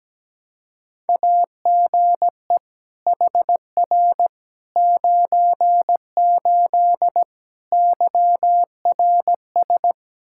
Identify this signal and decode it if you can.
Morse code, heard as a keyed tone, about 17 wpm, 700 Hz. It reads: AGE HR 98 YRS
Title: K1USN SST QSO Practice 17wpm